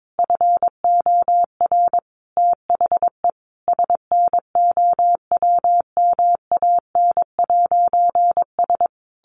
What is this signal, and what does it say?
Morse code, heard as a keyed tone, about 22 wpm, 700 Hz. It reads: FOR THE SNOWMAN'S
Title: Easy Phrases 22wpm